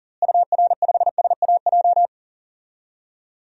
Morse code, heard as a keyed tone, about 40 wpm, 700 Hz. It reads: UR5SAJ
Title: DXCC-100 Call Signs - 6 Characters - Repeated 3x - 40wpm